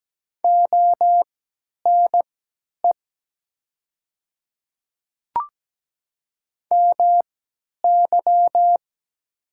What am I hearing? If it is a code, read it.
Morse code, heard as a keyed tone, about 17 wpm, spaced as for 10 wpm, 700 Hz. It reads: ONE MY
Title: 3x Character Spacing - Top 100 Words - 17wpm